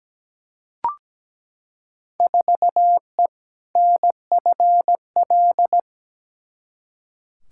Morse code, heard as a keyed tone, about 17 wpm, 700 Hz. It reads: 4E NFL
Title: Field Day Exchanges Encore - 17wpm